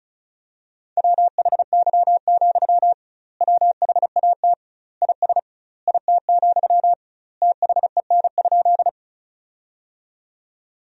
Morse code, heard as a keyed tone, about 36 wpm, 700 Hz. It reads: WHY, WHAT IS IT, THEN?
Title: 2x Word-Spacing - Sentences from Top 200 Words - 35wpm